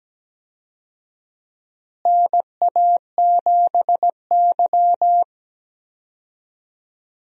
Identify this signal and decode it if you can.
Morse code, heard as a keyed tone, about 17 wpm, 700 Hz. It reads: NA7Y
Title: USA Call Signs - 4 Characters - Repeated 3x - 17wpm